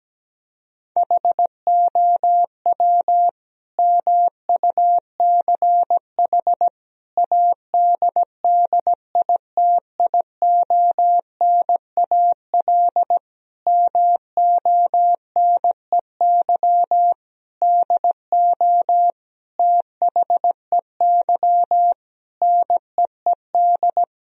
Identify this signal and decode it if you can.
Morse code, heard as a keyed tone, about 17 wpm, 700 Hz. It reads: HOW MUCH ADDITIONAL MONEY DO THEY NEED
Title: Sentences from Top 1000 Words - Review Part 4 - 17wpm